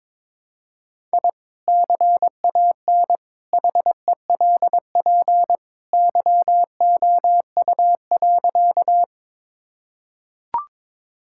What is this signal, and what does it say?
Morse code, heard as a keyed tone, about 22 wpm, 700 Hz. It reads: I CAN HELP YOU.
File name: practice / 22wpm / Sentences with 2-Word Phrases - 22wpm